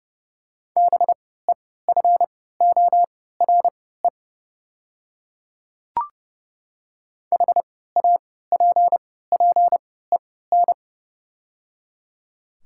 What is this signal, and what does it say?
Morse code, heard as a keyed tone, about 29 wpm, spaced as for 17 wpm, 700 Hz. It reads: BEFORE HAPPEN
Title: 3x Character Spacing - Top 500 Words - 30wpm